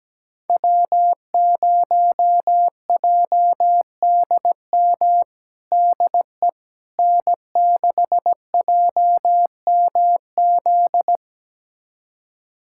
Morse code, heard as a keyed tone, about 17 wpm, 700 Hz. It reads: W0JDM DE N6JMZ
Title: USA Callsign DE Callsign 17wpm